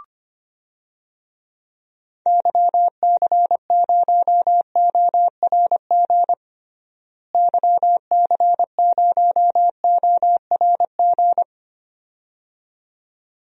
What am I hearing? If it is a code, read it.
Morse code, heard as a keyed tone, about 25 wpm, 700 Hz. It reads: YC0ORG YC0ORG
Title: DXCC-100 Call Signs - 6 Characters - Repeated 2x - 25wpm